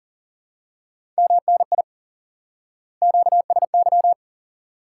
Morse code, heard as a keyed tone, about 40 wpm, 700 Hz. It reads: MNI QSY
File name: IQE Course 3 - Lesson 89 - YAGI - Two Elements - 40wpm